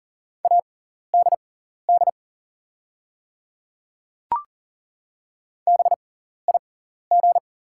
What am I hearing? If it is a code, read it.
Morse code, heard as a keyed tone, about 40 wpm, spaced as for 14 wpm, 700 Hz. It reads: ADD BIG
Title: 6x Character Spacing - Top 200 Words - 40wpm